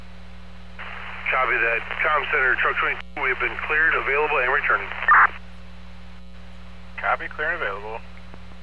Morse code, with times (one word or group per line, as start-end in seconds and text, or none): none